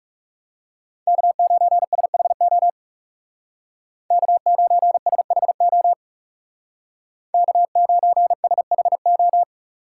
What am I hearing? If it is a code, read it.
Morse code, heard as a keyed tone, about 35 wpm, 700 Hz. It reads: K9SHO K9SHO K9SHO
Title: Speed Racing - USA 1x3 Call Signs 30wpm